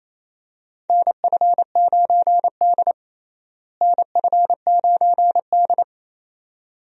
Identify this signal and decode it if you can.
Morse code, heard as a keyed tone, about 28 wpm, 700 Hz. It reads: NF9D NF9D
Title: USA Call Signs - 4 Characters - Repeated 2x - 28wpm